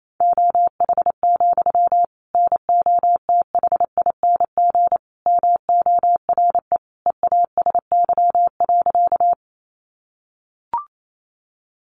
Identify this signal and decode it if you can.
Morse code, heard as a keyed tone, about 28 wpm, 700 Hz. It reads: OH, NOTHING MORE EASY.
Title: Sentences from Top 700 Words 28wpm